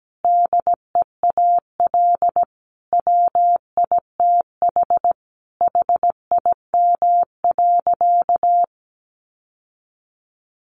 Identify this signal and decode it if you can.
Morse code, heard as a keyed tone, about 17 wpm, 700 Hz. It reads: DEAL WITH HIM.
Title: Sentences from Top 800 Words 17wpm